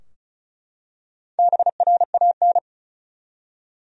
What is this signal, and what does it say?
Morse code, heard as a keyed tone, about 34 wpm, 700 Hz. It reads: BRAN
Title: USA Common Names - Max 5 Letters - 35wpm